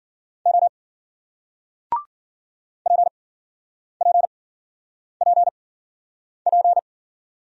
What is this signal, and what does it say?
Morse code, heard as a keyed tone, about 53 wpm, spaced as for 9 wpm, 700 Hz. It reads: O PPPP